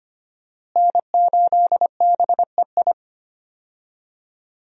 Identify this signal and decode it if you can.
Morse code, heard as a keyed tone, about 25 wpm, 700 Hz. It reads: N8BEI
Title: USA Call Signs - 5 Characters - Repeated 3x - 25wpm